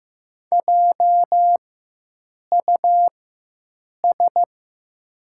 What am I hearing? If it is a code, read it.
Morse code, heard as a keyed tone, about 15 wpm, spaced as for 7 wpm, 700 Hz. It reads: JUS
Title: 4x Character Spacing - Top 200 Words - 15wpm